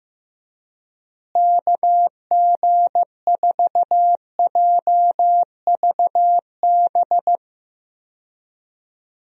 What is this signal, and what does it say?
Morse code, heard as a keyed tone, about 15 wpm, 700 Hz. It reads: KG4JVB
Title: USA Call Signs - 6 Characters - Repeated 2x - 15wpm